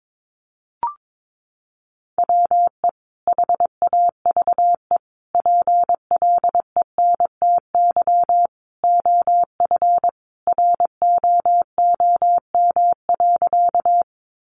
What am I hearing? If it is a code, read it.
Morse code, heard as a keyed tone, about 22 wpm, 700 Hz. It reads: WE HAVE PLENTY OF ROOM.
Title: Sentences from Top 1300 Words 22wpm